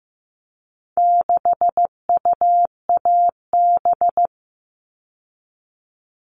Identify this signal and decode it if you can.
Morse code, heard as a keyed tone, about 15 wpm, 700 Hz. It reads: 6UAB
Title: All Cut Numbers - 4-Digit 15wpm